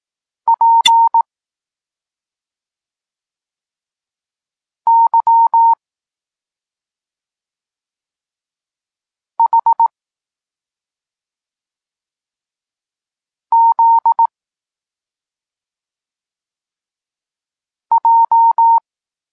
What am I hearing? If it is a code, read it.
Morse code, heard as a keyed tone, about 18 wpm, 920 Hz. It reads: P Y H Z J